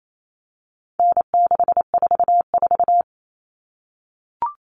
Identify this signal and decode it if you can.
Morse code, heard as a keyed tone, about 28 wpm, 700 Hz. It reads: N644
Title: All Cut Numbers - 4-Digit 28wpm